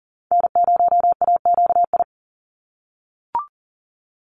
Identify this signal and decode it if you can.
Morse code, heard as a keyed tone, about 40 wpm, 690 Hz. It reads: N0AQI
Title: USA Call Signs 40wpm